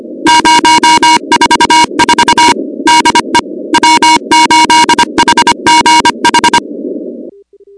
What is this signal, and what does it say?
Morse code, heard as a keyed tone, about 25 wpm, 950 Hz, with noise. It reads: 044 DE W8HGH
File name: contests / 2019-cw-wpx-kt5j-k5pi- -2330 0-16